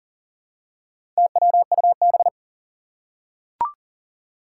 Morse code, heard as a keyed tone, about 41 wpm, 700 Hz. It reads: TWUB
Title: Four Challenging Letter-Number Combinations 40wpm